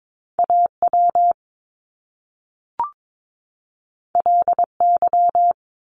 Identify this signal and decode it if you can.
Morse code, heard as a keyed tone, about 22 wpm, 700 Hz. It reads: AW LY
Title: Top Two Letter Combinations in Words 22wpm